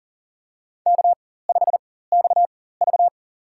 Morse code, heard as a keyed tone, about 40 wpm, spaced as for 19 wpm, 700 Hz. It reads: K5XV
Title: USA Call Signs - 4 Characters - 4x Character Spacing - 40wpm